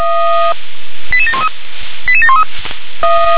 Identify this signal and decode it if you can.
Five-tone selective call (ZVEI), heard as quick runs of five tones, tones of about 70 ms. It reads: 88713 88713